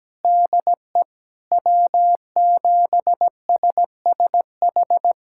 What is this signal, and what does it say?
Morse code, heard as a keyed tone, about 17 wpm, 700 Hz. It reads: DE W7SSH
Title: USA Callsign DE Callsign - 5 Characters - 17wpm